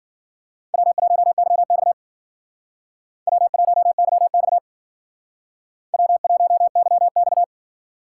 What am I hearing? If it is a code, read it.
Morse code, heard as a keyed tone, about 52 wpm, 700 Hz. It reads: W1YX W1YX W1YX